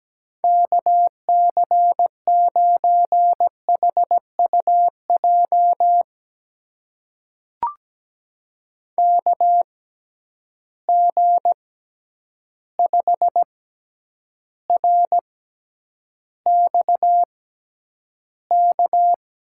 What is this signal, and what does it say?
Morse code, heard as a keyed tone, about 17 wpm, spaced as for 9 wpm, 700 Hz. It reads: KC9HUJ KG5RXK
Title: USA Call Signs - 6x Character Spacing - 17wpm